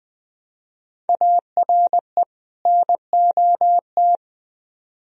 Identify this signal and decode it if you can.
Morse code, heard as a keyed tone, about 20 wpm, 700 Hz. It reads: ARE NOT
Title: Speed-Racing - Sets of 2 Words 17wpm